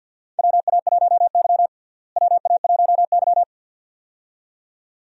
Morse code, heard as a keyed tone, about 50 wpm, 700 Hz. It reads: WA1Y WA1Y
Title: USA Call Signs - 4 Characters - Repeated 2x - 50wpm